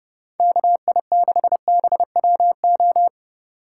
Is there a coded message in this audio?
Yes: Morse code from 0.4 to 3.1 s, heard as a keyed tone, about 30 wpm, 700 Hz. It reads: KI6BWO